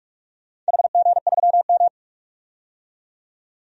Morse code, heard as a keyed tone, about 46 wpm, 700 Hz. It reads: HG3M
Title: DXCC-100 Call Signs - 4 Characters - 45wpm